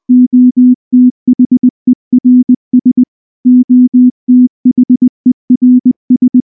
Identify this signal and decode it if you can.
Morse code, heard as a keyed tone, about 19 wpm, 250 Hz. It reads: OTHERS OTHERS
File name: morse / morse-training-20WPM-6